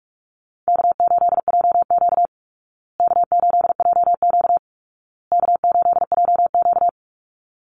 Morse code, heard as a keyed tone, about 47 wpm, 700 Hz. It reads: K8JQ K8JQ K8JQ
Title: Easy Call Signs Repeated 3x - 45wpm